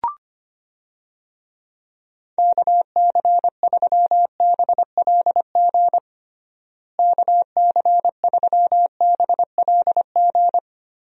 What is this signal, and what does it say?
Morse code, heard as a keyed tone, about 25 wpm, 700 Hz. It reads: KC3BLG KC3BLG
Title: USA Call Signs - 6 Characters - Repeated 3x - 25wpm